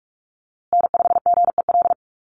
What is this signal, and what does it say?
Morse code, heard as a keyed tone, about 44 wpm, 700 Hz. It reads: N5GEL
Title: USA Call Signs Repeated 2x 45wpm